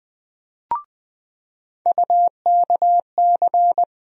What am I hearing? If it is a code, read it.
Morse code, heard as a keyed tone, about 20 wpm, 700 Hz. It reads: UKC
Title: USA Call Sign Postfixes - 3 Letters 20wpm